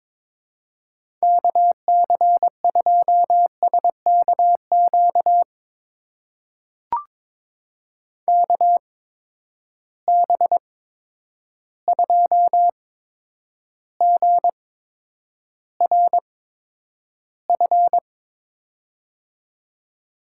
Morse code, heard as a keyed tone, about 22 wpm, spaced as for 9 wpm, 700 Hz. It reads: KC2SKQ KB2GRF
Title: USA Call Signs - 6 Characters - 8x Character Spacing - 22wpm